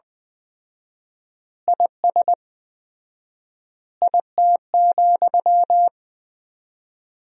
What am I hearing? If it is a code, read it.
Morse code, heard as a keyed tone, about 20 wpm, 700 Hz. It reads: IS IT,
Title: 4x Word-Spacing - Sentences from Top 300 Words - 20wpm